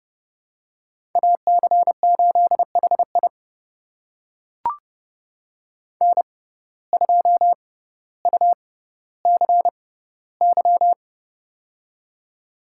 Morse code, heard as a keyed tone, about 30 wpm, spaced as for 15 wpm, 700 Hz. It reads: AC8HI N2UCY
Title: USA Call Signs - 5 Characters - 6x Character Spacing - 30wpm